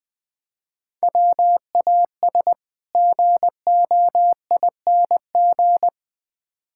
Morse code, heard as a keyed tone, about 20 wpm, 700 Hz. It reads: WAS GOING
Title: Build A Phrase - 3-Word Phrases - 20wpm